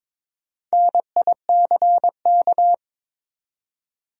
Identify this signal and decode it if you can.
Morse code, heard as a keyed tone, about 22 wpm, 700 Hz. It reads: NICK
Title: USA Common Names - Max 4 Letters - 22wpm